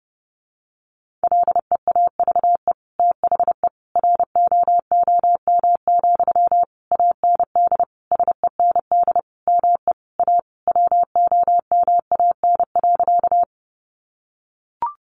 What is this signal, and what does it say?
Morse code, heard as a keyed tone, about 30 wpm, 700 Hz. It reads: LEAVE THE ROOM, AND SEND ME A WOMAN.
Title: Sentences from Top 1500 Words 30wpm